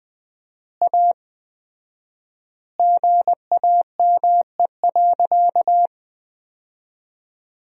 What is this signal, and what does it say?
Morse code, heard as a keyed tone, about 20 wpm, 700 Hz. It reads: A GAME.
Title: 4x Word-Spacing - Sentences from Top 500 Words - 20wpm